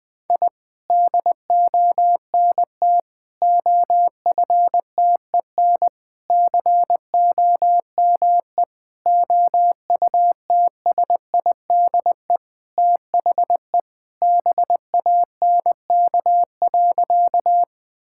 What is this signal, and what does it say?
Morse code, heard as a keyed tone, about 20 wpm, 700 Hz. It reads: I DONT OFTEN COME OUTSIDE THE BANK.
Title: Sentences from Top 1000 Words - Review Part 6 - 20wpm